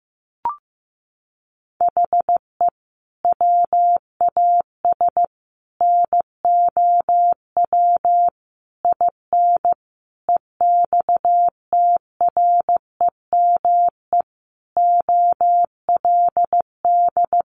Morse code, heard as a keyed tone, about 15 wpm, 700 Hz. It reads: HE WAS NOW IN EXTREME OLD
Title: Sentences from Top 1300 Words 15wpm